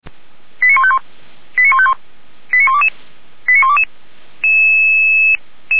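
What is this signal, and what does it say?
Five-tone selective call (ZVEI), heard as quick runs of five tones, tones of about 70 ms. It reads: 89151 89151 89122 89122
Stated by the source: ZVEI five-tone call received on an SDR, from the Feuerwehr Orlamünde fire brigade's server.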